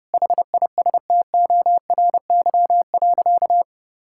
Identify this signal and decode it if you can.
Morse code, heard as a keyed tone, about 30 wpm, 700 Hz. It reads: HISTORY.
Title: Sentences from Top 1000 Words - Review Part 5 - 30wpm